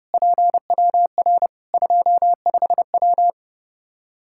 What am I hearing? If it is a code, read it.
Morse code, heard as a keyed tone, about 30 wpm, 700 Hz. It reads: PWR 25W
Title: Practice QSO - Exchange 2 - Info - 30wpm